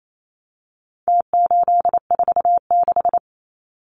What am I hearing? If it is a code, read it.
Morse code, heard as a keyed tone, about 28 wpm, 700 Hz. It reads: T846
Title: Numbers - Contest Style - 4-Digit 28wpm